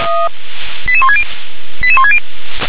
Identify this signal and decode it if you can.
Five-tone selective call (ZVEI), heard as quick runs of five tones, tones of about 70 ms. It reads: 88160 88160